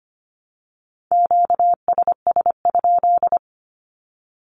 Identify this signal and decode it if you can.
Morse code, heard as a keyed tone, about 25 wpm, 700 Hz. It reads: QSS?
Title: Q-Codes Repeated 3x 25wpm